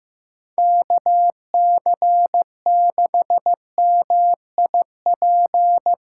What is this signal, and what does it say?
Morse code, heard as a keyed tone, about 15 wpm, 700 Hz. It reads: KC6MIP
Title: Speed Racing - USA 2x3 Call Signs 15wpm